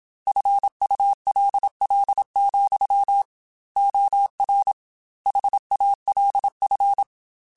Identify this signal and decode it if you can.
Morse code, heard as a keyed tone, about 27 wpm, 800 Hz. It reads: FULL, OR HALF